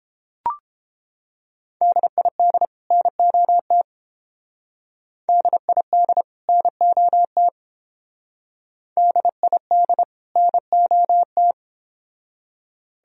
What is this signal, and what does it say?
Morse code, heard as a keyed tone, about 28 wpm, 700 Hz. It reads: DID NOT DID NOT DID NOT